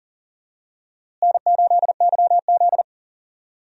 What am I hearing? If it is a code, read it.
Morse code, heard as a keyed tone, about 40 wpm, 700 Hz. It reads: N8YZ
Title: USA Call Signs - 4 Characters - Repeated 3x - 40wpm